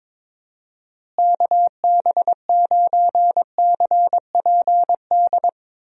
Morse code, heard as a keyed tone, about 22 wpm, 700 Hz. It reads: KB9CPD
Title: USA Call Signs - 6 Characters - Repeated 3x - 22wpm